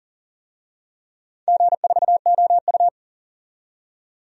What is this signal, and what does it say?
Morse code, heard as a keyed tone, about 38 wpm, 690 Hz. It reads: G4OU
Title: DXCC-100 Call Signs - 4 Characters - 40wpm